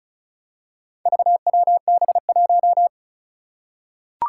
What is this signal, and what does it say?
Morse code, heard as a keyed tone, about 36 wpm, 700 Hz. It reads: VWB1
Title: Four Challenging Letter-Number Combinations 35wpm